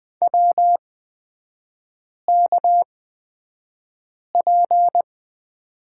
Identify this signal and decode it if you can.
Morse code, heard as a keyed tone, about 20 wpm, spaced as for 5 wpm, 690 Hz. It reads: WKP
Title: 229 Lesson 37e - Letter x - Three Characters 20wpm